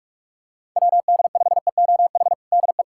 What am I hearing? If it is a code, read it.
Morse code, heard as a keyed tone, about 41 wpm, 700 Hz. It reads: WD5EOH DE
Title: USA Callsign DE Callsign - 6 Characters - 45wpm